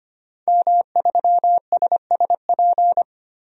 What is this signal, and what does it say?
Morse code, heard as a keyed tone, about 25 wpm, 700 Hz. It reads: M3SSP
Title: DXCC-100 Call Signs - 5 Characters - Repeated 3x - 25wpm